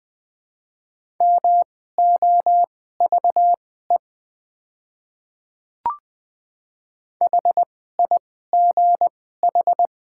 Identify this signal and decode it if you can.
Morse code, heard as a keyed tone, about 20 wpm, spaced as for 14 wpm, 700 Hz. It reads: MOVE HIGH
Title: 2x Character Spacing - Top 200 Words - 20wpm